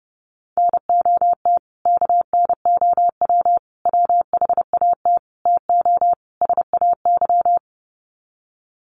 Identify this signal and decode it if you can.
Morse code, heard as a keyed tone, about 30 wpm, 700 Hz. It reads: NOT KNOW WHAT TO SAY
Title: Sets of 5 Words 30wpm